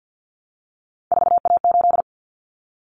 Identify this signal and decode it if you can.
Morse code, heard as a keyed tone, about 48 wpm, 700 Hz. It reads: 4A8
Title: Common Cut Numbers - 3-Digit 50wpm